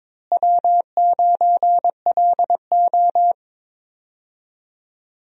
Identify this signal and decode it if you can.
Morse code, heard as a keyed tone, about 21 wpm, 700 Hz. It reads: W9LO
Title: USA Call Signs - 4 Characters - Repeated 2x - 22wpm